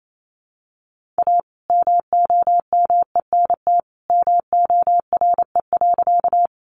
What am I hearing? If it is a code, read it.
Morse code, heard as a keyed tone, about 28 wpm, 700 Hz. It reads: A MOMENT MORE.
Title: Sentences from Top 1000 Words 28wpm